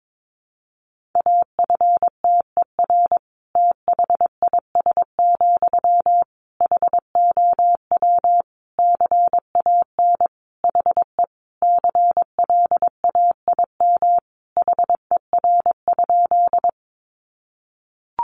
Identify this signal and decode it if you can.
Morse code, heard as a keyed tone, about 22 wpm, 700 Hz. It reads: AFTER THIS, HOW CAN HE CLAIM HER?
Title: Sentences from Top 1500 Words 22wpm